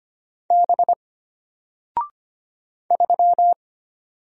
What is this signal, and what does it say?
Morse code, heard as a keyed tone, about 25 wpm, 700 Hz. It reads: B 3
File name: Single-Letter-Number-25wpm